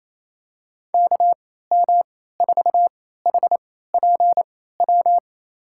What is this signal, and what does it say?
Morse code, heard as a keyed tone, about 28 wpm, spaced as for 16 wpm, 700 Hz. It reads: KM4HPW
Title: USA Call Signs - 6 Characters - 3x Character Spacing - 28wpm